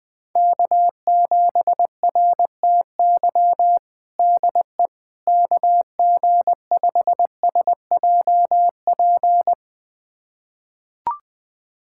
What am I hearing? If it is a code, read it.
Morse code, heard as a keyed tone, about 20 wpm, 700 Hz. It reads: K7RTY DE KG5SJP
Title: USA Callsign DE Callsign 20wpm